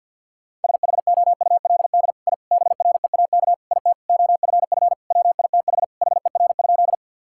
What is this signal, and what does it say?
Morse code, heard as a keyed tone, about 50 wpm, 700 Hz. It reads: SHOULD I BREAK IT OFF WITH HER?
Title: Sentences from Top 1000 Words - Review Part 6 - 50wpm